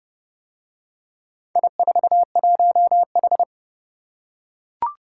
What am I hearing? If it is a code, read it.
Morse code, heard as a keyed tone, about 28 wpm, 700 Hz. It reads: I41H